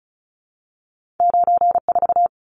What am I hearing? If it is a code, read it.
Morse code, heard as a keyed tone, about 36 wpm, 700 Hz. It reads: 94
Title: Numbers - 2-Digit 35wpm